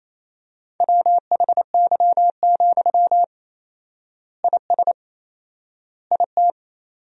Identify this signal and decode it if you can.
Morse code, heard as a keyed tone, about 28 wpm, 700 Hz. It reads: WHY, IS IT
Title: 4x Word-Spacing - Sentences from Top 300 Words - 28wpm